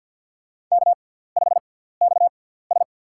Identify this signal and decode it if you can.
Morse code, heard as a keyed tone, about 50 wpm, spaced as for 17 wpm, 700 Hz. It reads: K5XS
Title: USA Call Signs - 4 Characters - 6x Character Spacing - 50wpm